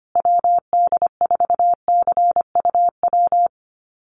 Morse code, heard as a keyed tone, about 25 wpm, 700 Hz. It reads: WD4CUW
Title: USA Call Signs Repeated 3x 25wpm